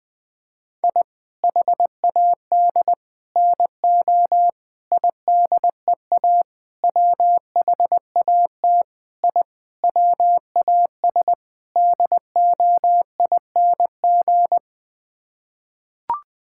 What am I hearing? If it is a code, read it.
Morse code, heard as a keyed tone, about 20 wpm, 700 Hz. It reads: I HAD NO IDEA WHAT I WAS DOING
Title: Sets of 8 Words - 20wpm